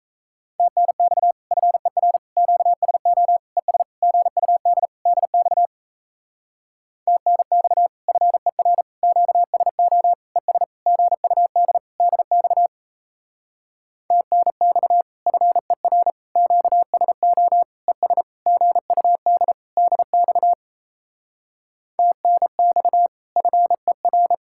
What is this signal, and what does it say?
Morse code, heard as a keyed tone, about 36 wpm, 700 Hz. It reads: TNX FER QSO ES GUD DX TNX FER QSO ES GUD DX TNX FER QSO ES GUD DX TNX FER